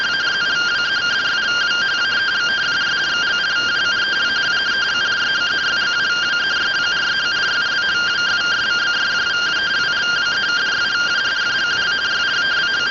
Two-tone teleprinter signal, about 170 Hz shift.